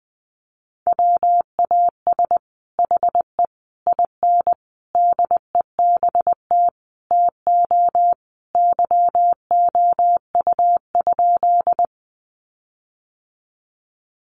Morse code, heard as a keyed tone, about 19 wpm, 700 Hz. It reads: WAS HE IN DEBT TO YOU?
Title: Sentences from Top 1300 Words 20wpm